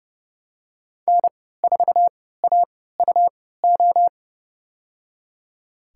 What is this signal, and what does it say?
Morse code, heard as a keyed tone, about 30 wpm, spaced as for 17 wpm, 700 Hz. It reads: N4AUO
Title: USA Call Signs - 5 Characters - 3x Character Spacing - 30wpm